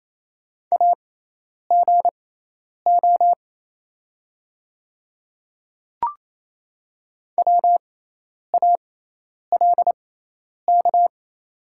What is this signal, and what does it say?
Morse code, heard as a keyed tone, about 28 wpm, spaced as for 10 wpm, 700 Hz. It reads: AGO WALK
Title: 6x Character Spacing - Top 500 Words - 28wpm